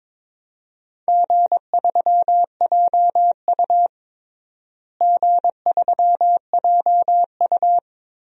Morse code, heard as a keyed tone, about 22 wpm, 700 Hz. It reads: G3JU G3JU
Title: DXCC-100 Call Signs - 4 Characters - Repeated 3x - 22wpm